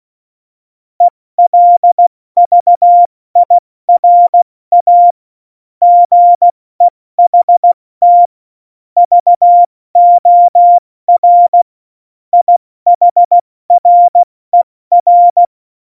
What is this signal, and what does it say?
Morse code, heard as a keyed tone, about 16 wpm, spaced as for 14 wpm, 700 Hz. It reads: ELVIRA GEHT VOR IHRER
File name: Morsekurs / Witze / Witz 25. Juni 2019, 70 F80